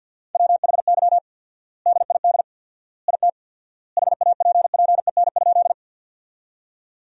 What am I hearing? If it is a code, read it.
Morse code, heard as a keyed tone, about 50 wpm, 700 Hz. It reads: WHY DID IT HAPPEN?
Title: 4x Word-Spacing - Sentences from Top 400 Words - 50wpm